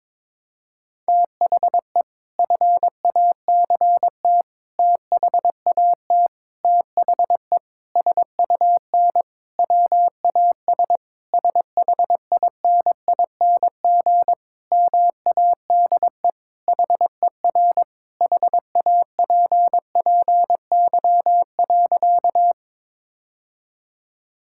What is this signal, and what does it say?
Morse code, heard as a keyed tone, about 22 wpm, 700 Hz. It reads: THE FACT THAT THE SUN WAS SHINING MADE HER HAPPY.
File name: practice / 22wpm / Sentences with 4-Word Phrases - 22wpm